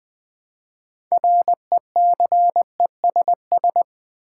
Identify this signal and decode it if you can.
Morse code, heard as a keyed tone, about 20 wpm, 700 Hz. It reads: RECESS
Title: Build A Word - 3 Letters At A Time - 9-Letter Words - 20wpm